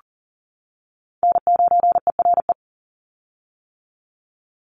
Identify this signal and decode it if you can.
Morse code, heard as a keyed tone, about 40 wpm, 700 Hz. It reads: N9ERE
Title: USA Call Signs 40wpm